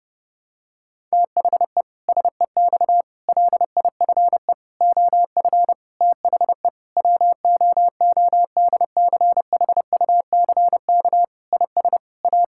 Morse code, heard as a keyed tone, about 30 wpm, 700 Hz. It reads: THE SEX LIFE OF THE WOODCHUCK IS A
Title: Pangram Challenge v2 - 30wpm